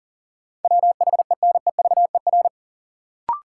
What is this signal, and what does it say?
Morse code, heard as a keyed tone, about 40 wpm, 700 Hz. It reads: WHENEVER
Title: Sentences from Top 1200 Words - New Word Drill - 40wpm